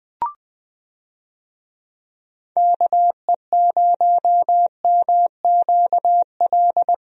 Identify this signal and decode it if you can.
Morse code, heard as a keyed tone, about 21 wpm, 700 Hz. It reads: KE0MQL
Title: USA Call Signs - 6 Characters - 20wpm